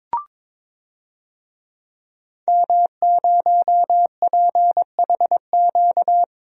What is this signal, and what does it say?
Morse code, heard as a keyed tone, about 22 wpm, 700 Hz. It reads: M0PHQ
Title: DXCC-100 Call Signs - 5 Characters - Repeated 3x - 22wpm